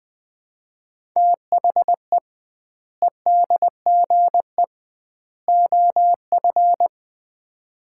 Morse code, heard as a keyed tone, about 21 wpm, 700 Hz. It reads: THE EDGE OF
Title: 2x Word-Spacing - Sets of 3 Words - 20wpm